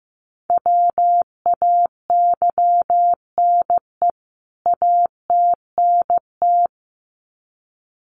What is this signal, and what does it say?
Morse code, heard as a keyed tone, about 15 wpm, 700 Hz. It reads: WAYNE ATNT
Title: CWT Exchange 15wpm